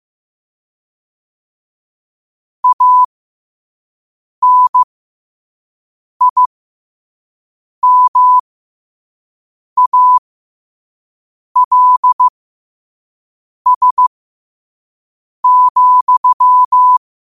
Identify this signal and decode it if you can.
Morse code, heard as a keyed tone, about 15 wpm, spaced as for 5 wpm, 1000 Hz. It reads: ANIMALS,
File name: qotdfiles / 5wpm